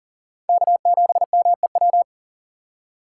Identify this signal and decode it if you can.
Morse code, heard as a keyed tone, about 40 wpm, 700 Hz. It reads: K7MEW
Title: USA Call Signs - 5 Characters - 6x Character Spacing - 40wpm